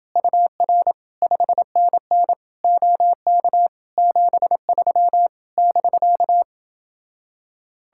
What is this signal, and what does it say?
Morse code, heard as a keyed tone, about 28 wpm, 700 Hz. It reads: UR 5NN OK 73 <BK>